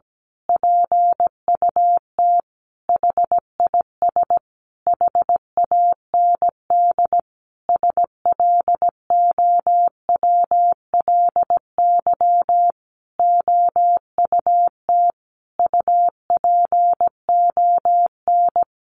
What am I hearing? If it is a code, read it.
Morse code, heard as a keyed tone, about 17 wpm, 700 Hz. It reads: PUT HIS HAND SLOWLY OUT UPON
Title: A Sound Of Thunder 17wpm